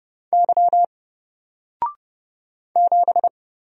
Morse code, heard as a keyed tone, about 29 wpm, 700 Hz. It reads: Y 7
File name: Single-Letter-Number-30wpm